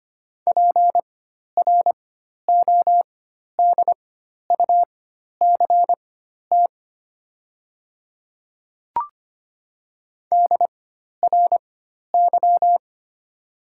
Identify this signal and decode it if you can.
Morse code, heard as a keyed tone, about 25 wpm, spaced as for 12 wpm, 690 Hz. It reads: PRODUCT DRY
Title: 4x Character Spacing - Top 500 Words - 25wpm